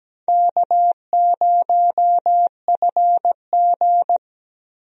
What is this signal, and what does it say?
Morse code, heard as a keyed tone, about 17 wpm, 700 Hz. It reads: K0FG
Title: USA Call Signs - 4 Characters - 17wpm